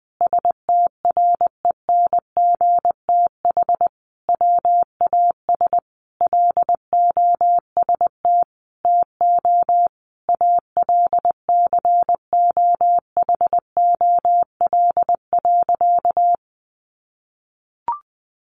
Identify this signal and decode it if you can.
Morse code, heard as a keyed tone, about 20 wpm, 700 Hz. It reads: STRENGTH WAS LOST TO ALCOHOL.
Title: Sentences from Top 1400 Words 20wpm